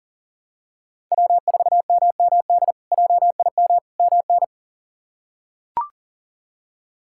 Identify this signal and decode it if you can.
Morse code, heard as a keyed tone, about 40 wpm, 700 Hz. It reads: W4MMD JIM MN
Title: K1USN SST QSO Practice - 40wpm